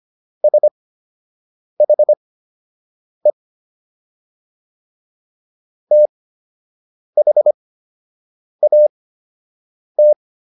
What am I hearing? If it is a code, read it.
Morse code, heard as a keyed tone, about 25 wpm, spaced as for 7 wpm, 600 Hz. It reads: SHE THAT